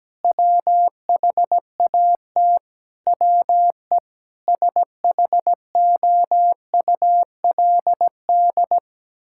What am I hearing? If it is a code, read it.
Morse code, heard as a keyed tone, about 17 wpm, 700 Hz. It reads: WHAT WE SHOULD
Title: Sentences from Top 1600 Words 17wpm